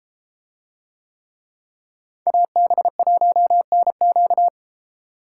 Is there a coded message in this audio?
Yes: Morse code from 2.3 to 4.5 s, heard as a keyed tone, about 32 wpm, 700 Hz. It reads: AB1NQ